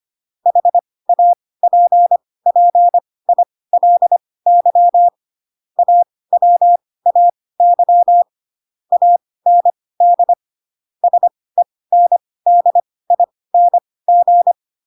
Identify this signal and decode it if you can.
Morse code, heard as a keyed tone, about 25 wpm, spaced as for 18 wpm, 700 Hz. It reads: HAPPILY AWAY AND SENDING
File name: Pinocchio ch3 12 25 ik1hge